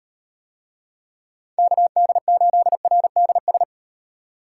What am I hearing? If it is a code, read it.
Morse code, heard as a keyed tone, about 37 wpm, 700 Hz. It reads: KD8RDS